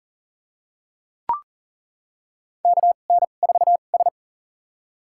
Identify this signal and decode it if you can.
Morse code, heard as a keyed tone, about 38 wpm, spaced as for 28 wpm, 700 Hz. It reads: KN4S
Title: USA Call Signs - 4 Characters - 2x Character Spacing - 40wpm